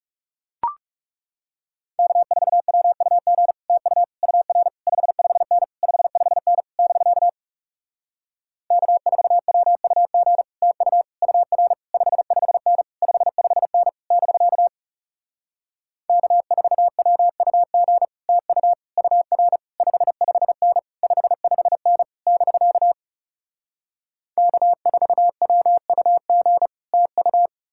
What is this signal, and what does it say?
Morse code, heard as a keyed tone, about 40 wpm, 700 Hz. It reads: K4WUG TU UR 55N 55N <BK> K4WUG TU UR 55N 55N <BK> K4WUG TU UR 55N 55N <BK> K4WUG TU